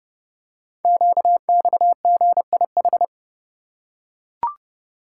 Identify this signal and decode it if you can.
Morse code, heard as a keyed tone, about 30 wpm, 700 Hz. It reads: QXGIH
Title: Code Groups - 5 Letters - 30wpm